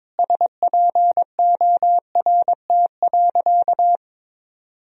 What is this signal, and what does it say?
Morse code, heard as a keyed tone, about 22 wpm, 700 Hz. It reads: SPORT.
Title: Sentences from Top 1000 Words - Review Part 4 - 22wpm